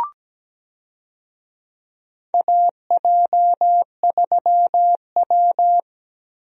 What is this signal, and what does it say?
Morse code, heard as a keyed tone, about 17 wpm, 700 Hz. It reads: AJ3W
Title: USA Call Signs - 4 Characters - Repeated 2x - 17wpm